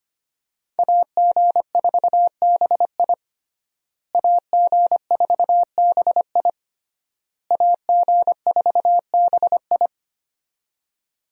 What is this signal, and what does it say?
Morse code, heard as a keyed tone, about 25 wpm, 700 Hz. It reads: AG4BI AG4BI AG4BI
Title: USA Call Signs - 5 Characters - Repeated 3x - 25wpm